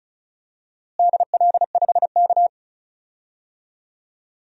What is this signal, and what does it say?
Morse code, heard as a keyed tone, about 36 wpm, 700 Hz. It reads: DL5K
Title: DXCC-100 Call Signs - 4 Characters - Repeated 2x - 35wpm